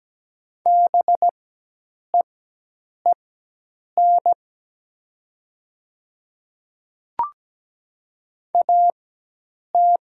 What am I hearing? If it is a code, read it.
Morse code, heard as a keyed tone, about 17 wpm, spaced as for 8 wpm, 700 Hz. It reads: BEEN AT
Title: 4x Character Spacing - Top 100 Words - 17wpm